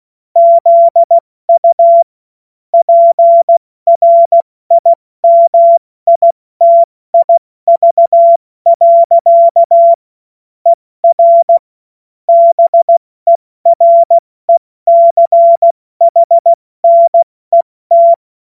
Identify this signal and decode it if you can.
Morse code, heard as a keyed tone, about 16 wpm, spaced as for 14 wpm, 680 Hz. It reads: ZU PRIMITIV. ER BERECHNET
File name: Morsekurs / Witze / Witz 07. Dezember 2021, 70 F80